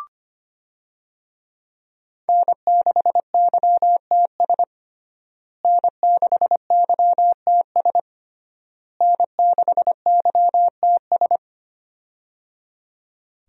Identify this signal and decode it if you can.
Morse code, heard as a keyed tone, about 25 wpm, 700 Hz. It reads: N6YTS N6YTS N6YTS
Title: USA Call Signs - 5 Characters - Repeated 3x - 25wpm